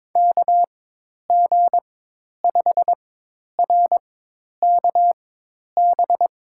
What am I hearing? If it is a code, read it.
Morse code, heard as a keyed tone, about 22 wpm, spaced as for 10 wpm, 700 Hz. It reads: KG5RKB
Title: USA Call Signs - 6 Characters - 4x Character Spacing - 22wpm